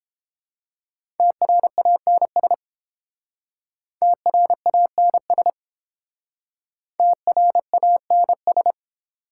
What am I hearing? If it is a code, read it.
Morse code, heard as a keyed tone, about 30 wpm, 690 Hz. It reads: TRANS TRANS TRANS